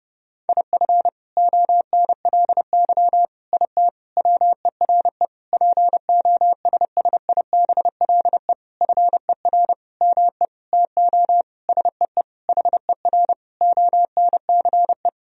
Morse code, heard as a keyed tone, about 30 wpm, 700 Hz. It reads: IF ONLY IT WERE POSSIBLE FER ME TO SEE HER ONCE
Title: Sentences from Top 500 Words and Abbreviations 30wpm